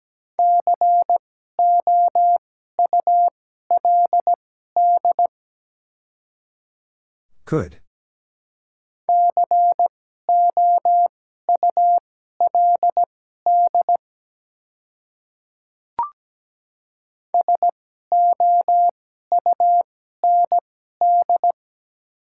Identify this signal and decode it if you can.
Morse code, heard as a keyed tone, about 17 wpm, spaced as for 12 wpm, 700 Hz. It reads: COULD COULD SOUND
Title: 2x Character Spacing - Top 100 Words - 17wpm